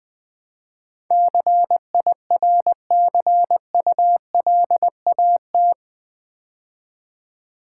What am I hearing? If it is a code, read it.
Morse code, heard as a keyed tone, about 20 wpm, 700 Hz. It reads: CIRCULAT
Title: Build A Word - 4 Letters At A Time - 11-Letter Words - 20wpm